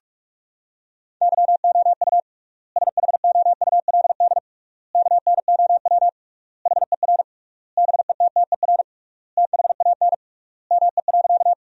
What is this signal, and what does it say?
Morse code, heard as a keyed tone, about 46 wpm, 700 Hz. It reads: YOU SHOULD KNOW HER BETTER THAN ME.
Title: 3x Word-Spacing - Sentences from Top 400 Words - 45wpm